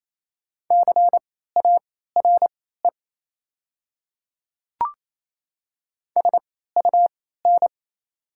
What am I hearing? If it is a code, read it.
Morse code, heard as a keyed tone, about 28 wpm, spaced as for 16 wpm, 700 Hz. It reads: CARE SUN
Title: 3x Character Spacing - Top 300 Words - 28wpm